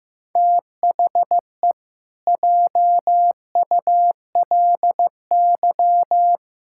Morse code, heard as a keyed tone, about 15 wpm, 700 Hz. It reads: THE JULY
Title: Pangram Challenge v2 - 15wpm